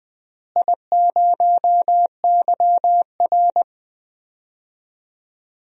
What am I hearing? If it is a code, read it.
Morse code, heard as a keyed tone, about 20 wpm, 700 Hz. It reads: I0YR
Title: DXCC-100 Call Signs - 4 Characters - Repeated 3x - 20wpm